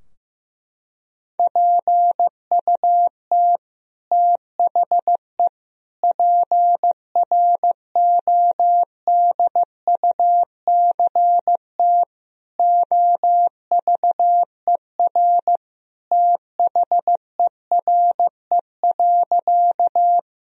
Morse code, heard as a keyed tone, about 15 wpm, 700 Hz. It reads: PUT THE PRODUCT OVER THERE.
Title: Sentences from Top 1000 Words - Review Part 2 - 15wpm